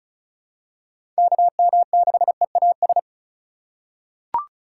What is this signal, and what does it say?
Morse code, heard as a keyed tone, about 36 wpm, 700 Hz. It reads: KM6EAS
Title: USA Call Signs - 3x Character Spacing - 35wpm